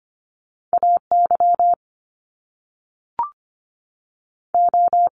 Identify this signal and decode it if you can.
Morse code, heard as a keyed tone, about 25 wpm, 700 Hz. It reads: A Y O
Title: Top Two Letter Combinations in Words 25wpm